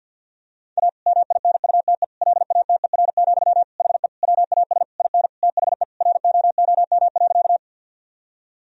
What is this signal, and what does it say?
Morse code, heard as a keyed tone, about 50 wpm, 700 Hz. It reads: A MINUTE LATER, HE WAS IN THE ROOM.
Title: Sentences from Top 1000 Words - Review Part 4 - 50wpm